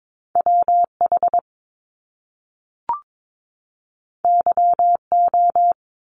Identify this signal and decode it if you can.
Morse code, heard as a keyed tone, about 22 wpm, 700 Hz. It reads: WH YO
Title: Top Two Letter Combinations in Words 22wpm